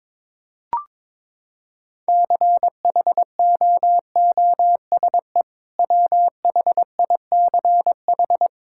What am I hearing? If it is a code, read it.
Morse code, heard as a keyed tone, about 22 wpm, 700 Hz. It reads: CHOOSE WHICH
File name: practice / 22wpm / Sentences with 2-Word Phrases - 22wpm